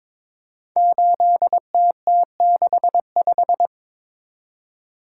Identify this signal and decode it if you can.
Morse code, heard as a keyed tone, about 22 wpm, 700 Hz. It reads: 8TT65
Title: Code Groups - 5 Letter:Numbers - 22wpm